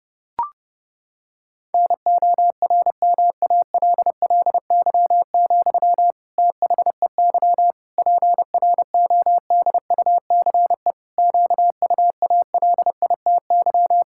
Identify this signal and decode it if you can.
Morse code, heard as a keyed tone, about 30 wpm, 700 Hz. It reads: NORMALLY, THEY PRODUCE QUALITY
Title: Sentences from Top 1000 Words - Review Part 6 - 30wpm